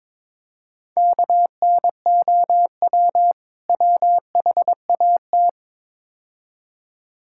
Morse code, heard as a keyed tone, about 22 wpm, 700 Hz. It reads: KNOW WHAT
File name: practice / 22wpm / Build A Phrase - 5-Word Phrases - 22wpm